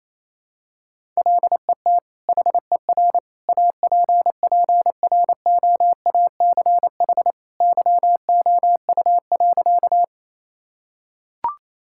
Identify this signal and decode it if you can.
Morse code, heard as a keyed tone, about 28 wpm, 700 Hz. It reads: LET HER APPROACH YOU.
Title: Sentences from Top 1000 Words - Review Part 6 - 28wpm